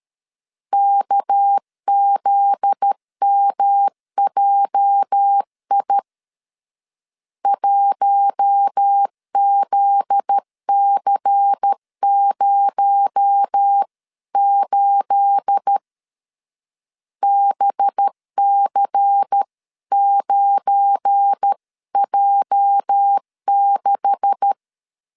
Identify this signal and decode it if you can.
Morse code, heard as a keyed tone, about 13 wpm, 800 Hz. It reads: KZMJI 1ZC08 BC9J6
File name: CW / DET / 1 9 1500C S B 12wpm 3819